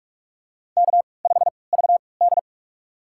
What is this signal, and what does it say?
Morse code, heard as a keyed tone, about 44 wpm, spaced as for 25 wpm, 690 Hz. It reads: K5VD
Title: USA Call Signs - 4 Characters - 3x Character Spacing - 45wpm